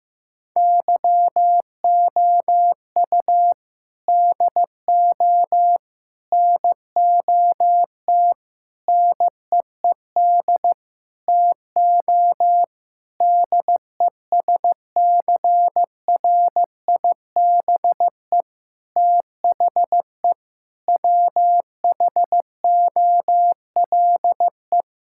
Morse code, heard as a keyed tone, about 15 wpm, 700 Hz. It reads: YOU DO NOT NEED TO DESCRIBE THE WHOLE
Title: Sentences from Top 1600 Words 15wpm